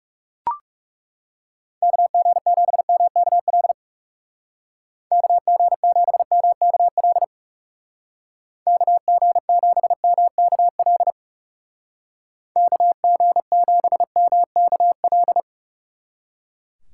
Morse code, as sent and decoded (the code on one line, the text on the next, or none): -.- --. --... -- -.- .-.. / -.- --. --... -- -.- .-.. / -.- --. --... -- -.- .-.. / -.- --. --... -- -.- .-..
KG7MKL KG7MKL KG7MKL KG7MKL